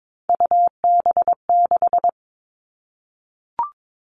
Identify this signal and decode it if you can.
Morse code, heard as a keyed tone, about 22 wpm, 700 Hz. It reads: UB6
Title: All Cut Numbers - 3-Digit 22wpm